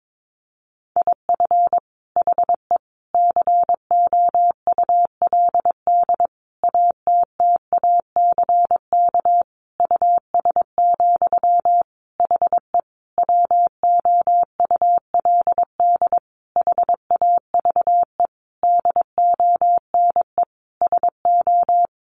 Morse code, heard as a keyed tone, about 22 wpm, 700 Hz. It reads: IF HE COULD ATTACK US, HE WOULD HAVE DONE SO
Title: Sentences from Top 1500 Words 22wpm